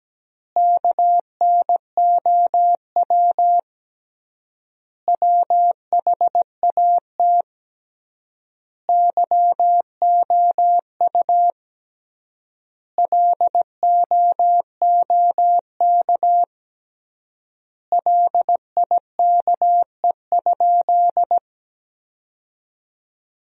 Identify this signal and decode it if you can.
Morse code, heard as a keyed tone, about 17 wpm, 700 Hz. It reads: KNOW WHAT YOU LOOK LIKE?
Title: 3x Word-Spacing - Sentences from Top 100 Words - 17wpm